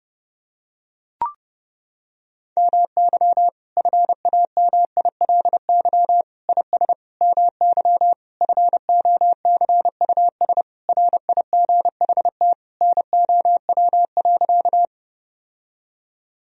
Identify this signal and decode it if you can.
Morse code, heard as a keyed tone, about 30 wpm, 700 Hz. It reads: MY FAMILY IS MY FOCUS RIGHT NOW.
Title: Sentences from Top 1000 Words - Review Part 4 - 30wpm